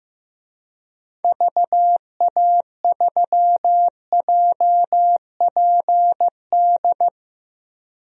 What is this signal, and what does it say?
Morse code, heard as a keyed tone, about 15 wpm, 700 Hz. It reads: VA3JPD
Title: DXCC-100 Call Signs - 6 Characters - 15wpm